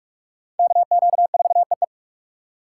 Morse code, heard as a keyed tone, about 40 wpm, 700 Hz. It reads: KQ4EE